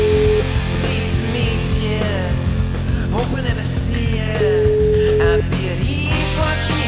Ringback tone (Europe, 425 Hz), with noise.